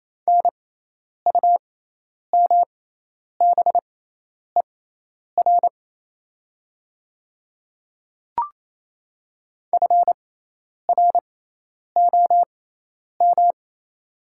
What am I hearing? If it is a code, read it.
Morse code, heard as a keyed tone, about 27 wpm, spaced as for 10 wpm, 700 Hz. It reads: NUMBER FROM
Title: 6x Character Spacing - Top 200 Words - 28wpm